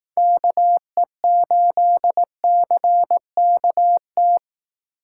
Morse code, heard as a keyed tone, about 18 wpm, 700 Hz. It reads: KE8CKT
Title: Speed Racing - USA Call Signs 15wpm